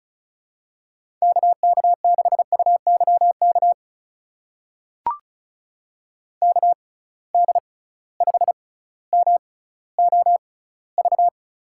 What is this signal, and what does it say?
Morse code, heard as a keyed tone, about 36 wpm, spaced as for 18 wpm, 700 Hz. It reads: KK6UYK KD5MOV
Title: USA Call Signs - 6 Characters - 6x Character Spacing - 35wpm